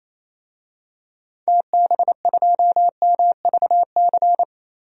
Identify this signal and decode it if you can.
Morse code, heard as a keyed tone, about 28 wpm, 700 Hz. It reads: TB2MVC
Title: DXCC-100 Call Signs - 6 Characters - 28wpm